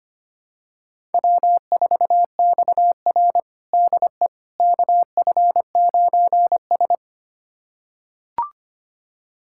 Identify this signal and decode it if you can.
Morse code, heard as a keyed tone, about 25 wpm, 700 Hz. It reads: W4XR DE KF9S
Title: USA Callsign DE Callsign - 4 Characters - 25wpm